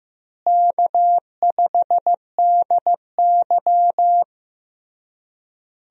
Morse code, heard as a keyed tone, about 15 wpm, 700 Hz. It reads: K5DY
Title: USA Call Signs - 4 Characters - Repeated 3x - 15wpm